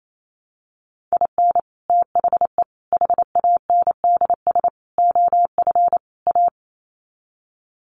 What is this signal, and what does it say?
Morse code, heard as a keyed tone, about 28 wpm, 700 Hz. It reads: IN THE HANDS OF A
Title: Sets of 5 Words Encore 28wpm